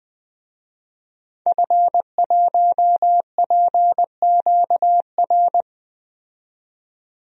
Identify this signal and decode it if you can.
Morse code, heard as a keyed tone, about 20 wpm, 700 Hz. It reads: F1PQR
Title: DXCC-100 Call Signs - 5 Characters - 20wpm